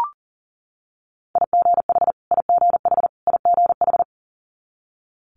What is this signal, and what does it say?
Morse code, heard as a keyed tone, about 40 wpm, 700 Hz. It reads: IGH IGH IGH
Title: Multi-letter Phonemes Repeated 3x 40wpm